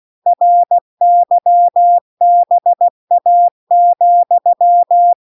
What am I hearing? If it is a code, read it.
Morse code, heard as a keyed tone, about 16 wpm, 700 Hz. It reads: RYBA,